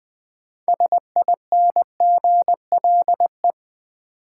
Morse code, heard as a keyed tone, about 20 wpm, 700 Hz. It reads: SINGLE
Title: Sentences from Top 700 Words - New Word Drill - 20wpm